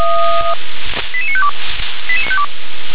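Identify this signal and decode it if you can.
Five-tone selective call (ZVEI), heard as quick runs of five tones, tones of about 70 ms. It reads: 88952
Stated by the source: Five-tone ZVEI selective call received by SDR.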